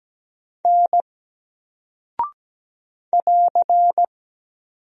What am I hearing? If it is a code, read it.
Morse code, heard as a keyed tone, about 17 wpm, 700 Hz. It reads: N <AR>
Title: Single Character (Letter:Number:Punctuation:Prosign) Alt 17wpm